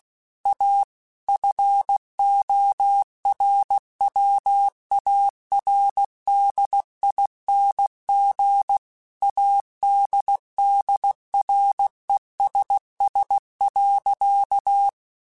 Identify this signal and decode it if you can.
Morse code, heard as a keyed tone, about 16 wpm, 800 Hz. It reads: A FORWARDING ADDRESS.